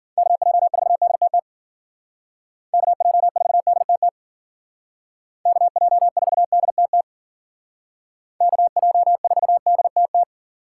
Morse code, heard as a keyed tone, about 52 wpm, 700 Hz. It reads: KJ4DTT KJ4DTT KJ4DTT KJ4DTT